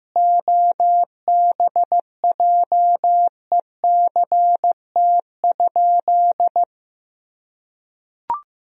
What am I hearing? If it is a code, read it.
Morse code, heard as a keyed tone, about 16 wpm, 700 Hz. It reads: OBJECT?
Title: Sentences from Top 1000 Words - Review Part 3 - 15wpm